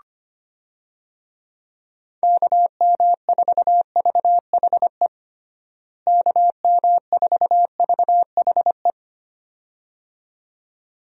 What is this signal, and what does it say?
Morse code, heard as a keyed tone, about 25 wpm, 700 Hz. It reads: KM4VHE KM4VHE
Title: USA Call Signs - 6 Characters - Repeated 2x - 25wpm